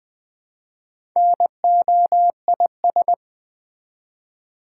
Morse code, heard as a keyed tone, about 20 wpm, 700 Hz. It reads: NOIS
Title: Build A Word - 1 Letter At A Time - 5-Letter Words - 20wpm